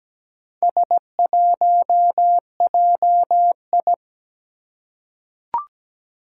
Morse code, heard as a keyed tone, about 17 wpm, 700 Hz. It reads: S1JI
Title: Four Challenging Letter-Number Combinations 17wpm